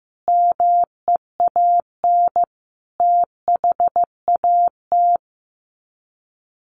Morse code, heard as a keyed tone, about 15 wpm, 700 Hz. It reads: MEAN THAT
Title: Sets of 5 Words Encore 15wpm